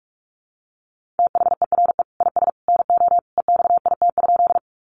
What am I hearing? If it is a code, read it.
Morse code, heard as a keyed tone, about 44 wpm, 700 Hz. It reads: THERE IS NO EXIT?
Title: Sentences from Top 1100 Words 45wpm